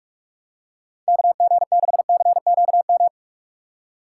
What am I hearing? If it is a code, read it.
Morse code, heard as a keyed tone, about 46 wpm, 700 Hz. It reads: KG6CQM
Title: USA Call Signs - 6 Characters - 3x Character Spacing - 45wpm